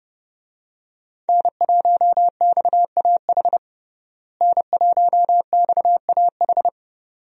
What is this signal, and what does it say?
Morse code, heard as a keyed tone, about 30 wpm, 700 Hz. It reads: N1XAH N1XAH
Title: USA Call Signs - 5 Characters - Repeated 2x - 30wpm